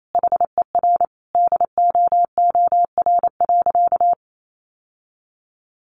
Morse code, heard as a keyed tone, about 28 wpm, 700 Hz. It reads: HER DOOR.
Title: Sentences from Top 1400 Words 28wpm